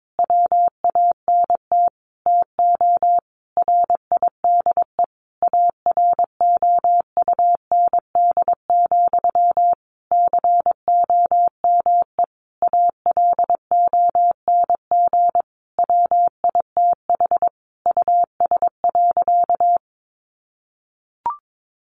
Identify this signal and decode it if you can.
Morse code, heard as a keyed tone, about 22 wpm, 700 Hz. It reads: WANT TO RIDE AROUND, COME ALONG WITH US.
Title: Sentences from Top 1200 Words 22wpm